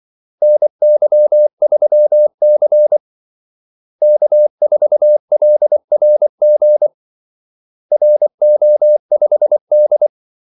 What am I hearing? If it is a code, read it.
Morse code, heard as a keyed tone, about 24 wpm, 600 Hz. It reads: NY3C K4LRG RO5D
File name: Callsigns-24wpm